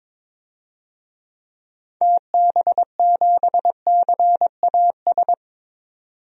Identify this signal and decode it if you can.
Morse code, heard as a keyed tone, about 21 wpm, 700 Hz. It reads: TB7CAS
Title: DXCC-100 Call Signs - 6 Characters - 22wpm